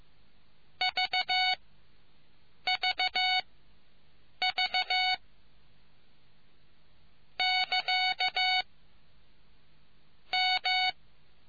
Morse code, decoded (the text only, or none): VVV <KA> M